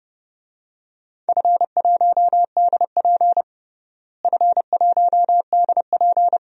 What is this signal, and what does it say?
Morse code, heard as a keyed tone, about 31 wpm, 700 Hz. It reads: F1DP F1DP
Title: DXCC-100 Call Signs - 4 Characters - Repeated 2x - 30wpm